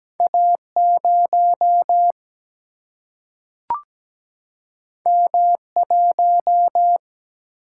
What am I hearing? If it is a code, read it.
Morse code, heard as a keyed tone, about 17 wpm, 700 Hz. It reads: A 0 M 1
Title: Single-Letter Single-Number 17wpm